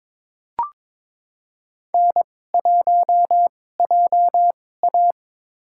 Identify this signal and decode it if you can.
Morse code, heard as a keyed tone, about 22 wpm, spaced as for 16 wpm, 700 Hz. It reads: N1JA
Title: USA Call Signs - 4 Characters - 2x Character Spacing - 22wpm